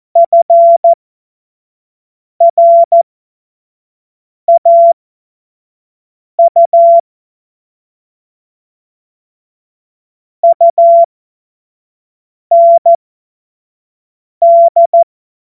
Morse code, heard as a keyed tone, about 14 wpm, spaced as for 5 wpm, 680 Hz. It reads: FRAU UND
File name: Morsekurs / Witze / Witz 16. Februar 2021, 25 F70